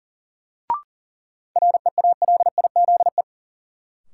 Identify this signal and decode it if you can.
Morse code, heard as a keyed tone, about 40 wpm, 700 Hz. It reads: REALIZE
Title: Sentences from Top 1000 Words - New Word Drill - 40wpm